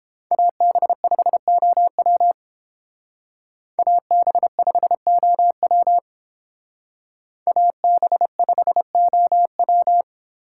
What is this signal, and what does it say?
Morse code, heard as a keyed tone, about 30 wpm, 700 Hz. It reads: AB5OW AB5OW AB5OW